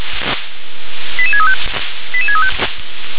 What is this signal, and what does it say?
Five-tone selective call (ZVEI), heard as quick runs of five tones, tones of about 70 ms. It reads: 88635 88635